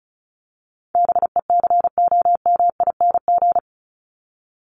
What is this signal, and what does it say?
Morse code, heard as a keyed tone, about 34 wpm, 700 Hz. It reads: BECOMING
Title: ING Suffix - Top 100 Words with Base Words 35wpm